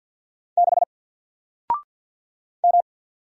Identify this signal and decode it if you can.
Morse code, heard as a keyed tone, about 50 wpm, 700 Hz. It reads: 6 M
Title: Single Letter-Number 50wpm